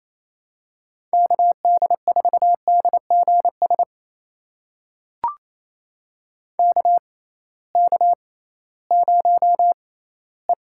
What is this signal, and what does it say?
Morse code, heard as a keyed tone, about 28 wpm, spaced as for 16 wpm, 700 Hz. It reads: KD4DGS KK0E
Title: USA Call Signs - 6x Character Spacing - 28wpm